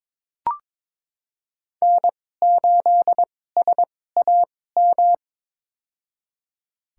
Morse code, heard as a keyed tone, about 22 wpm, spaced as for 16 wpm, 700 Hz. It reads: N8SAM
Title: USA Call Signs - 5 Characters - 2x Character Spacing - 22wpm